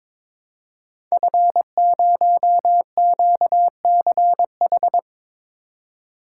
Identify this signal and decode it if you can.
Morse code, heard as a keyed tone, about 22 wpm, 690 Hz. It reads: F0QCH
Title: Code Groups - 5 Letter:Numbers - 22wpm